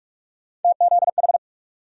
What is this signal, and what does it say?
Morse code, heard as a keyed tone, about 45 wpm, 700 Hz. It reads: TZH
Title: USA Call Sign Postfixes - 3 Letters 45wpm